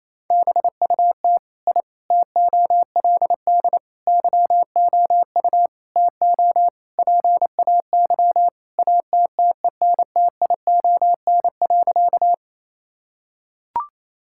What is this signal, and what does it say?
Morse code, heard as a keyed tone, about 28 wpm, 700 Hz. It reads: BUT I TOLD YOU TO PAY ATTENTION.
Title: Sentences from Top 1000 Words - Review Part 6 - 28wpm